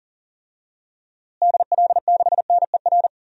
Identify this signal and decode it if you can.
Morse code, heard as a keyed tone, about 40 wpm, 700 Hz. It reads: DL6NER
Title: DXCC-100 Call Signs - 6 Characters - Repeated 3x - 40wpm